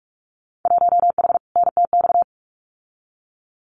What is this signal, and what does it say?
Morse code, heard as a keyed tone, about 44 wpm, 700 Hz. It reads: 1H NTX
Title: Winter Field Day Exchanges 45wpm